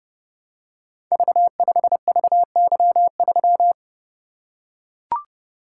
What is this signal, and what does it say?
Morse code, heard as a keyed tone, about 28 wpm, 700 Hz. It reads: V5VY3
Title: Code Groups - 5 Letter:Numbers - 30wpm